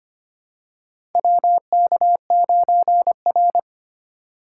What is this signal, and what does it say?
Morse code, heard as a keyed tone, about 25 wpm, 700 Hz. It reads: WK9R
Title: USA Call Signs - 3x Character Spacing - 25wpm